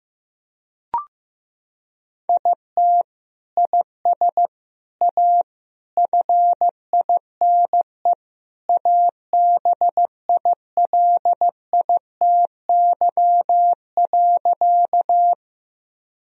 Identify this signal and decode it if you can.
Morse code, heard as a keyed tone, about 16 wpm, 700 Hz. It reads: IT IS A FINE ABILITY.
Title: Sentences from Top 1000 Words - Review Part 3 - 15wpm